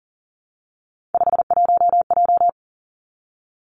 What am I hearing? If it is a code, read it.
Morse code, heard as a keyed tone, about 42 wpm, 700 Hz. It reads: 51J
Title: Three Challenging Letter-Number Combinations 40wpm